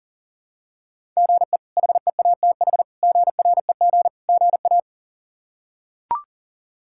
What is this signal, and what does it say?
Morse code, heard as a keyed tone, about 37 wpm, 700 Hz. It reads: GE HEATH GREG GA
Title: K1USN SST QSO Practice - 40wpm